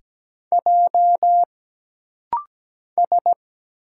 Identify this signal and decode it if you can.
Morse code, heard as a keyed tone, about 17 wpm, 700 Hz. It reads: J S
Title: Single Letters - Rapid-Fire 17wpm